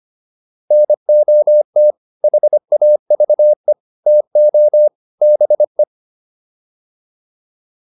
Morse code, heard as a keyed tone, about 25 wpm, 600 Hz. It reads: NOT HAVE TO BE
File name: WD304 25